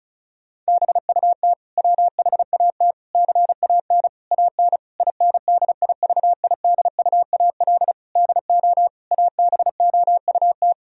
Morse code, heard as a keyed tone, about 34 wpm, 700 Hz. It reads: BUT WHAT CAN AN INDIVIDUAL DO ABOUT
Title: Sentences from Top 1000 Words - Review Part 4 - 35wpm